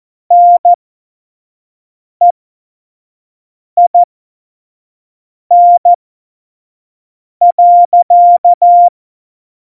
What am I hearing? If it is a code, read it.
Morse code, heard as a keyed tone, about 14 wpm, spaced as for 5 wpm, 700 Hz. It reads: NEIN.
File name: Morsekurs / Witze / Witz 04. Februar 2020, 25 F70